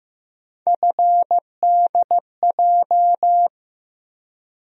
Morse code, heard as a keyed tone, about 15 wpm, 700 Hz. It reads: FDJ
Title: USA Call Sign Postfixes - 3 Letters 15wpm